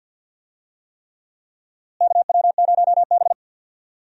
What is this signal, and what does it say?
Morse code, heard as a keyed tone, about 46 wpm, 700 Hz. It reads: KW0B
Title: USA Call Signs - 4 Characters - Repeated 3x - 50wpm